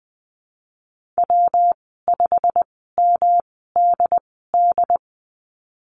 Morse code, heard as a keyed tone, about 20 wpm, spaced as for 15 wpm, 700 Hz. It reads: W5MDD
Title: 122 Lesson 20g - Number 3 - Call Signs 20wpm